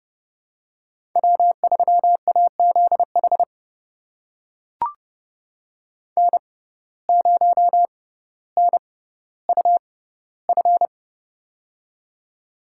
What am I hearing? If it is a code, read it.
Morse code, heard as a keyed tone, about 30 wpm, spaced as for 15 wpm, 700 Hz. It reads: W3AZH N0NUF
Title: USA Call Signs - 5 Characters - 6x Character Spacing - 30wpm